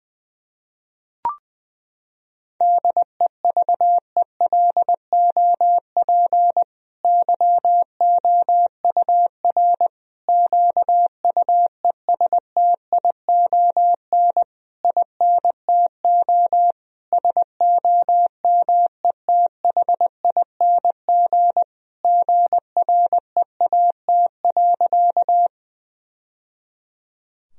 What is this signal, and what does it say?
Morse code, heard as a keyed tone, about 21 wpm, 700 Hz. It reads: DEVELOP YOUR QUESTION INTO SOMETHING GREAT.
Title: Sentences from Top 1000 Words - Review Part 2 - 20wpm